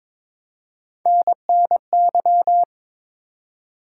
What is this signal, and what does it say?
Morse code, heard as a keyed tone, about 22 wpm, 700 Hz. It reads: NNY
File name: practice / 22wpm / ARRL Sections v2 - 22wpm